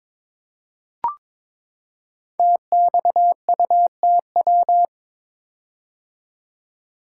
Morse code, heard as a keyed tone, about 22 wpm, 700 Hz. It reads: TXUTW
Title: Code Groups - 5 Letters - 22wpm